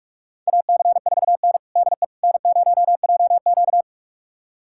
Morse code, heard as a keyed tone, about 46 wpm, 700 Hz. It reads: AC4N DE N0JQ
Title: USA Callsign DE Callsign - 4 Characters - 45wpm